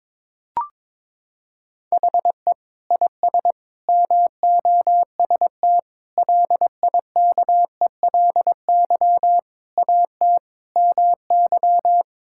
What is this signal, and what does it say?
Morse code, heard as a keyed tone, about 21 wpm, 700 Hz. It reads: HE IS MOST LIKELY AT MY
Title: Sentences from Top 1600 Words 22wpm